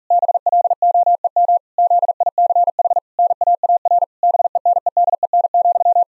Morse code, heard as a keyed tone, about 40 wpm, 700 Hz. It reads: BLOEM ZICH NAAR BENEDEN,
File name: Zendamateurnieuws en een stukje geseinde morse